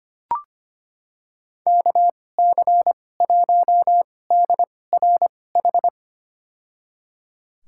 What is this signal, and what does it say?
Morse code, heard as a keyed tone, about 25 wpm, spaced as for 18 wpm, 700 Hz. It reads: KC1DRH
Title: USA Call Signs - 6 Characters - 2x Character Spacing - 25wpm